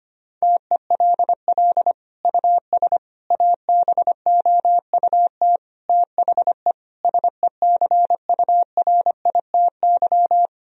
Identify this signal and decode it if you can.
Morse code, heard as a keyed tone, about 25 wpm, 700 Hz. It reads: TELL US ABOUT THE SECURITY
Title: Sentences from Top 1000 Words - Review Part 4 - 25wpm